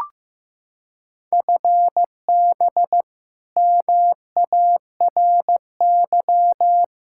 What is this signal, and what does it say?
Morse code, heard as a keyed tone, about 16 wpm, 700 Hz. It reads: FB MARY
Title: Practice QSO - Exchange 3 - More Info - 15wpm